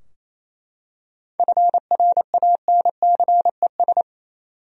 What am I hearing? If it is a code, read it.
Morse code, heard as a keyed tone, about 28 wpm, 700 Hz. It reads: FRANCES
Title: USA Common Names - 28wpm